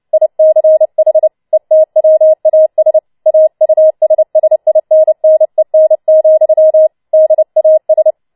Morse code, heard as a keyed tone, about 29 wpm, 630 Hz. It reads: ICH ETWAS AUSSINNEN, DAS
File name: Schneewittchen Teil3 30wpm